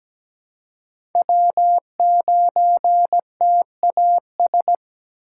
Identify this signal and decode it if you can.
Morse code, heard as a keyed tone, about 17 wpm, 700 Hz. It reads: W9TAS
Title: USA Call Signs - 5 Characters - 2x Character Spacing - 17wpm